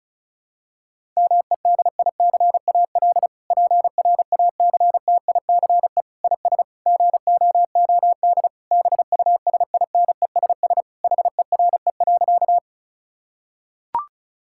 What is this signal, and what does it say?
Morse code, heard as a keyed tone, about 34 wpm, 700 Hz. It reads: MEDICAL PRACTICE IS GOOD BUSINESS HERE.
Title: Sentences from Top 1000 Words - Review Part 5 - 35wpm